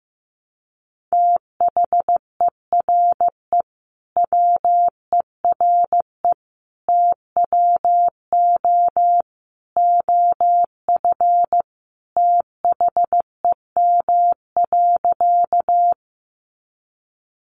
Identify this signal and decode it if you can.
Morse code, heard as a keyed tone, about 15 wpm, 700 Hz. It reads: THERE WERE TWO OF THEM.
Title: Sentences from Top 100 Words 15wpm